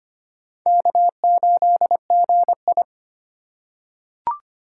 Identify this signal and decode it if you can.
Morse code, heard as a keyed tone, about 25 wpm, 700 Hz. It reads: K8GI
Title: USA Call Signs - 4 Characters - 2x Character Spacing - 25wpm